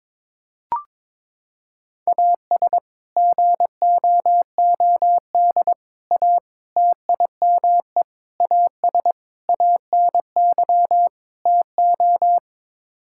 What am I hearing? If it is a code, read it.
Morse code, heard as a keyed tone, about 22 wpm, 700 Hz. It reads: AS GOOD A TIME AS ANY TO
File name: practice / 22wpm / Sets of 7 Words - 22wpm